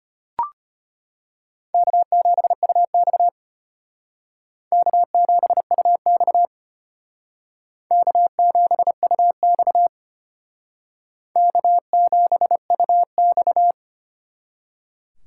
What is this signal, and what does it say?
Morse code, heard as a keyed tone, about 32 wpm, 700 Hz. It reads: K7UX K7UX K7UX K7UX